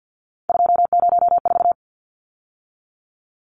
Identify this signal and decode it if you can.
Morse code, heard as a keyed tone, about 50 wpm, 700 Hz. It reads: 204
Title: Numbers - 3-Digit 50wpm